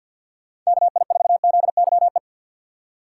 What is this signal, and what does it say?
Morse code, heard as a keyed tone, about 52 wpm, 700 Hz. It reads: KI4ZYE
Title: USA Call Signs - 6 Characters - 6x Character Spacing - 50wpm